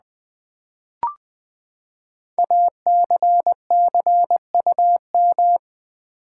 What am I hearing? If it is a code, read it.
Morse code, heard as a keyed tone, about 20 wpm, 700 Hz. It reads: ACCUM
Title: Build A Word - 5 Letters At A Time - 10-Letter Words - 20wpm